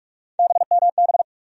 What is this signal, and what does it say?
Morse code, heard as a keyed tone, about 43 wpm, 700 Hz. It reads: BMB
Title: USA Call Sign Postfixes - 3 Letters 45wpm